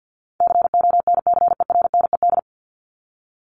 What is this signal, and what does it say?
Morse code, heard as a keyed tone, about 50 wpm, 700 Hz. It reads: CONCERNED
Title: Top 751-1000 Words 50wpm